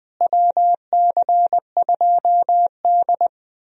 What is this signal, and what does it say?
Morse code, heard as a keyed tone, about 20 wpm, 700 Hz. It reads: WC2D
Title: USA Call Signs - 4 Characters - Repeated 3x - 20wpm